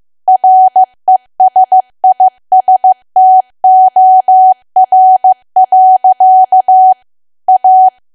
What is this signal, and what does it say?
Morse code, heard as a keyed tone, about 15 wpm, 750 Hz. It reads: RESISTOR. A
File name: cwa / W1AW-Level2 / 15WPM